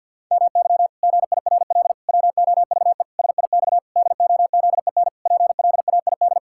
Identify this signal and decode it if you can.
Morse code, heard as a keyed tone, about 50 wpm, 700 Hz. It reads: MY GIRL WOVE SIX DOZEN PLAID
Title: Pangram Challenge v2 - 50wpm